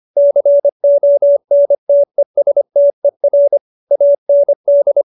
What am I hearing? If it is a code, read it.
Morse code, heard as a keyed tone, about 25 wpm, 570 Hz. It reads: CONTESTER AND